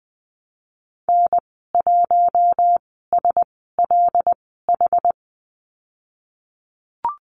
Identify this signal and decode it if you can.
Morse code, heard as a keyed tone, about 20 wpm, spaced as for 15 wpm, 700 Hz. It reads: N1SLH
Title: 076 Lesson 13g - Letter L - Call Signs 20wpm